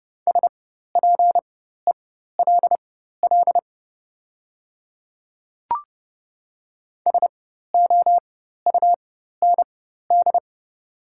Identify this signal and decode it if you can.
Morse code, heard as a keyed tone, about 30 wpm, spaced as for 14 wpm, 700 Hz. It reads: SPELL SOUND
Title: 4x Character Spacing - Top 200 Words - 30wpm